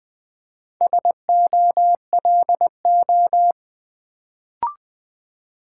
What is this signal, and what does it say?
Morse code, heard as a keyed tone, about 20 wpm, 700 Hz. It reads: SOLO
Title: Build A Word - 1 Letter At A Time - 4-Letter Words - 20wpm